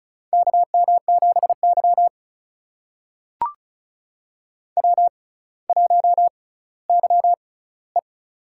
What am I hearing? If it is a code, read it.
Morse code, heard as a keyed tone, about 34 wpm, spaced as for 18 wpm, 700 Hz. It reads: KM7Y W1YE
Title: USA Call Signs - 4 Characters - 6x Character Spacing - 35wpm